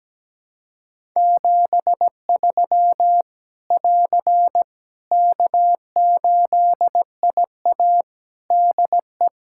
Morse code, heard as a keyed tone, about 17 wpm, 700 Hz. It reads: 73 <AR> K8IA DE
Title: Practice QSO - Exchange 4 - Ending - 17wpm